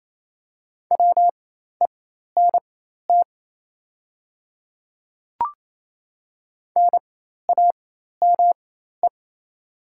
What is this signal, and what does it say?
Morse code, heard as a keyed tone, about 28 wpm, spaced as for 13 wpm, 700 Hz. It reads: WENT NAME
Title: 4x Character Spacing - Top 300 Words - 28wpm